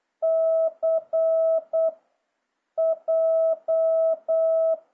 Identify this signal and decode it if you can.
Morse code, heard as a keyed tone, about 8 wpm, 650 Hz. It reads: C J